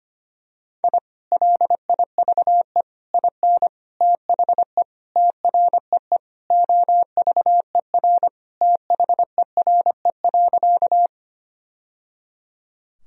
Morse code, heard as a keyed tone, about 25 wpm, 700 Hz. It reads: I LIVE IN THE TREE OVER THERE.
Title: Sentences from Top 1000 Words - Review Part 1 - 25wpm